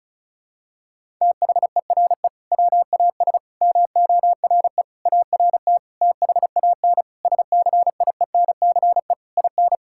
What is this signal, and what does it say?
Morse code, heard as a keyed tone, about 36 wpm, 700 Hz. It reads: THERE WAS MORE ART THAN SCIENCE IN
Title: Sentences from Top 1000 Words - Review Part 6 - 35wpm